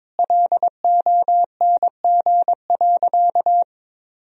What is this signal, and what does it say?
Morse code, heard as a keyed tone, about 22 wpm, 700 Hz. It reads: LONG.
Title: Sentences from Top 1000 Words - Review Part 2 - 22wpm